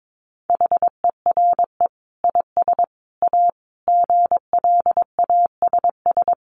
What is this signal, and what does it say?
Morse code, heard as a keyed tone, about 22 wpm, 700 Hz. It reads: HERE IS A GLASS
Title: Sentences from Top 1300 Words 22wpm